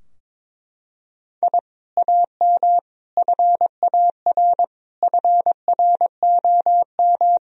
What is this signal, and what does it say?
Morse code, heard as a keyed tone, about 22 wpm, 700 Hz. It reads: I AM FAR FROM
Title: Sentences from Top 1000 Words - Review Part 3 - 22wpm